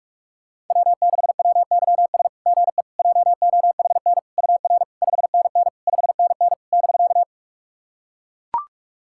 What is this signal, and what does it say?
Morse code, heard as a keyed tone, about 44 wpm, 690 Hz. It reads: W6WYS GE JOHN UR 5NN 5NN <BK>
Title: SOTA QSO Practice 45wpm